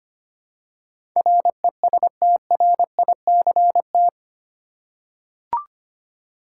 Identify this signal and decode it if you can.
Morse code, heard as a keyed tone, about 25 wpm, 700 Hz. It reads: RESTRICT
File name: 8 Letter Words - First 3 Letters Given - 25wpm